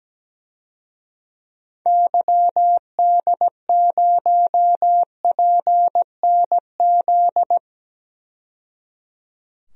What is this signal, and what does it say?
Morse code, heard as a keyed tone, about 17 wpm, 700 Hz. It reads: YD0PNZ
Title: DXCC-100 Call Signs - 6 Characters - 17wpm